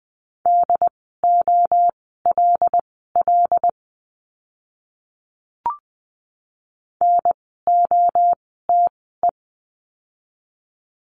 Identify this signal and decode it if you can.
Morse code, heard as a keyed tone, about 20 wpm, spaced as for 15 wpm, 700 Hz. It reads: DOLL NOTE
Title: 075 Lesson 13f - Letter L - Words 20wpm